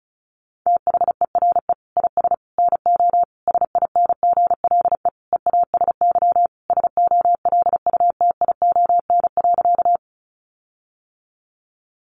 Morse code, heard as a keyed tone, about 34 wpm, 700 Hz. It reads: THERE IS NO SINGLE EASY SOLUTION.
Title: Sentences from Top 1300 Words 35wpm